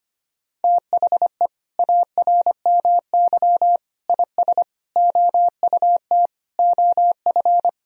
Morse code, heard as a keyed tone, about 25 wpm, 700 Hz. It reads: THE ARMY IS OUT OF
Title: Sentences from Top 1000 Words - Review Part 4 - 25wpm